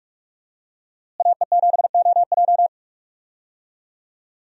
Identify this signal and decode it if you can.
Morse code, heard as a keyed tone, about 46 wpm, 700 Hz. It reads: AE7OJ
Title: USA Call Signs - 5 Characters - Repeated 3x - 45wpm